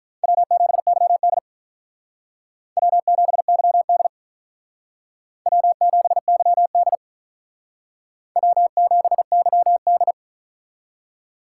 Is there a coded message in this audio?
Yes: Morse code from 0.2 to 10.1 s, heard as a keyed tone, about 44 wpm, 700 Hz. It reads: W7YD W7YD W7YD W7YD